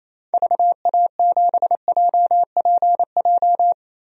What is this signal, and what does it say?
Morse code, heard as a keyed tone, about 28 wpm, 700 Hz. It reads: VA7JPJ
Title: DXCC-100 Call Signs - 6 Characters - 28wpm